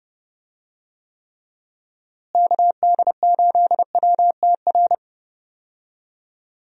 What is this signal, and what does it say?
Morse code, heard as a keyed tone, about 31 wpm, 700 Hz. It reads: KD8WTR
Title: USA Call Signs - 6 Characters - Repeated 3x - 30wpm